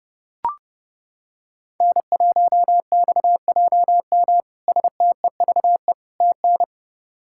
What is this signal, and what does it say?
Morse code, heard as a keyed tone, about 30 wpm, 700 Hz. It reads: N1XJM STEVE TN
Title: K1USN SST QSO Practice - 30wpm